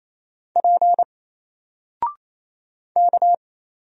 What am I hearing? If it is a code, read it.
Morse code, heard as a keyed tone, about 27 wpm, 700 Hz. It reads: P K
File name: Single-Letter-Number-28wpm